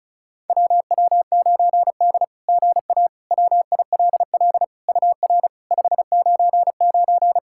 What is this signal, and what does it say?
Morse code, heard as a keyed tone, about 36 wpm, 700 Hz. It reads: WW9D GA WILL UR 599
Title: SOTA QSO Practice 35wpm